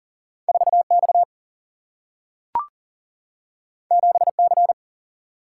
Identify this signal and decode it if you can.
Morse code, heard as a keyed tone, about 40 wpm, 700 Hz. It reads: 4X 7C
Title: Single-Letter Single-Number 40wpm